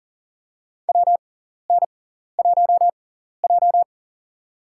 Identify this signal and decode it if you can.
Morse code, heard as a keyed tone, about 40 wpm, spaced as for 14 wpm, 700 Hz. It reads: WN1J
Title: USA Call Signs - 4 Characters - 6x Character Spacing - 40wpm